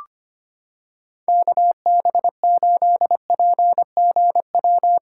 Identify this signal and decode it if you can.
Morse code, heard as a keyed tone, about 25 wpm, 700 Hz. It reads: KB8PGW
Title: K1USN SST QSO Practice - 25wpm